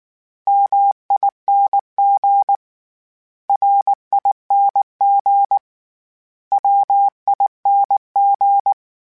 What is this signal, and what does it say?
Morse code, heard as a keyed tone, about 19 wpm, 800 Hz. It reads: MING RING WING